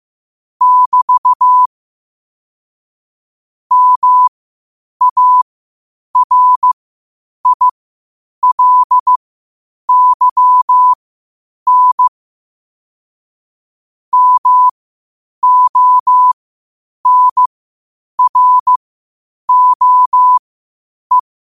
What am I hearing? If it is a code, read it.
Morse code, heard as a keyed tone, about 15 wpm, spaced as for 8 wpm, 1000 Hz. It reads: <BT> MARILYN MONROE
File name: qotdfiles / 8wpm